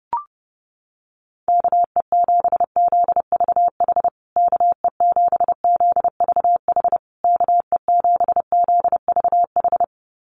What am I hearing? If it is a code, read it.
Morse code, heard as a keyed tone, about 30 wpm, 700 Hz. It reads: KE7ZVH KE7ZVH KE7ZVH
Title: USA Call Signs Repeated 3x 30wpm